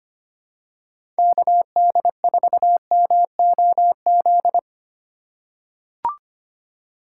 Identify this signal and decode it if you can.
Morse code, heard as a keyed tone, about 25 wpm, 700 Hz. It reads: KD4MOZ
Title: USA Call Signs - 6 Characters - 3x Character Spacing - 25wpm